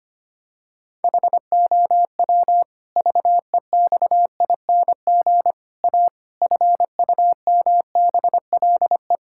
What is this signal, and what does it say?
Morse code, heard as a keyed tone, about 25 wpm, 700 Hz. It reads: HOW VEXING A FUMBLE
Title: Pangram Challenge v2 - 25wpm